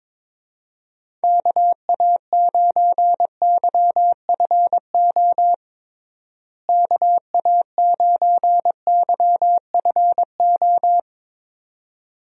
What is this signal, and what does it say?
Morse code, heard as a keyed tone, about 22 wpm, 700 Hz. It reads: KA9YFO KA9YFO
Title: USA Call Signs - 6 Characters - Repeated 2x - 22wpm